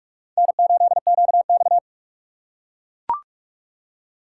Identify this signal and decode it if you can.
Morse code, heard as a keyed tone, about 43 wpm, 700 Hz. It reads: N8QX
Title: USA Call Signs - 4 Characters - 3x Character Spacing - 45wpm